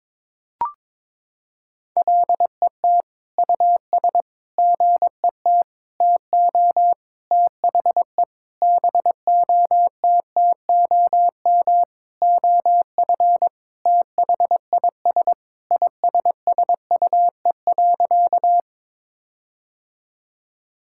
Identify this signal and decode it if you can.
Morse code, heard as a keyed tone, about 22 wpm, 700 Hz. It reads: LET US GET TO THE BOTTOM OF THIS ISSUE.
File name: Sentences with 7-Word Phrases - 22wpm